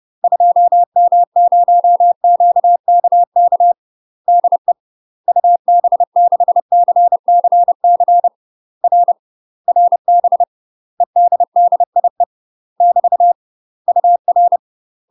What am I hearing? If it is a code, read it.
Morse code, heard as a keyed tone, about 30 wpm, 700 Hz. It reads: 2M0QKK DE UB6CCC R RB EDDIE <BT> UR
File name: UK MASTER QSOs TRAINING / 30WPM / UK MASTER QSOs 0012